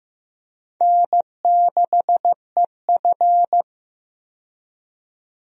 Morse code, heard as a keyed tone, about 15 wpm, 700 Hz. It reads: N6EF
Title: USA Call Signs - 4 Characters - 15wpm